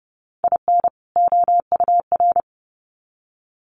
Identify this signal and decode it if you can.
Morse code, heard as a keyed tone, about 30 wpm, 700 Hz. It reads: IN OUR
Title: Sets of 2 Words 30wpm